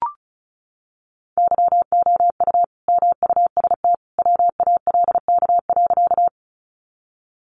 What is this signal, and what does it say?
Morse code, heard as a keyed tone, about 34 wpm, 700 Hz. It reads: YOU MUST WALK.
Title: Sentences from Top 300 Words 35wpm